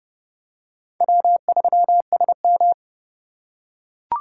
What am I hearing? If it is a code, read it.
Morse code, heard as a keyed tone, about 27 wpm, 700 Hz. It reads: W3SM